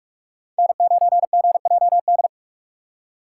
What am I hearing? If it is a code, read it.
Morse code, heard as a keyed tone, about 46 wpm, 700 Hz. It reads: N9GJD
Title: USA Call Signs - 5 Characters - 2x Character Spacing - 45wpm